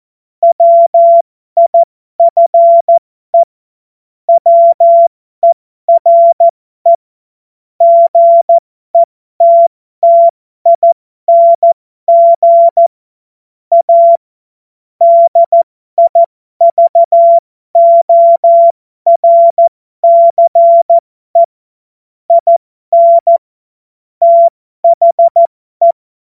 Morse code, heard as a keyed tone, about 14 wpm, spaced as for 12 wpm, 680 Hz. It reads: WIFE WERE GETTING A DIVORCE IN THE
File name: Morsekurs / Witze